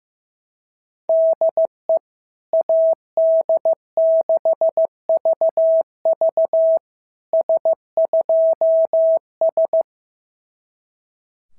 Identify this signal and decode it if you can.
Morse code, heard as a keyed tone, about 14 wpm, 650 Hz. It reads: DE AD6VV S2S
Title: SOTA QSO Practice 15wpm